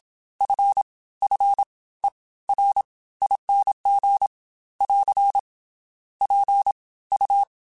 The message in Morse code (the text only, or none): FFERING <AR> PU